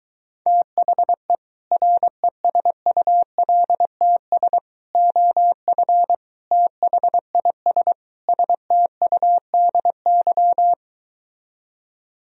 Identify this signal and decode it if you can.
Morse code, heard as a keyed tone, about 23 wpm, 700 Hz. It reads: THE RESULTS OF THIS STUDY